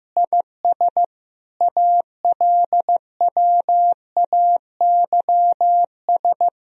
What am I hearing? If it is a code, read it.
Morse code, heard as a keyed tone, about 15 wpm, 700 Hz. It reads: IS ALWAYS
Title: Sentences from Top 1000 Words - Review Part 4 - 15wpm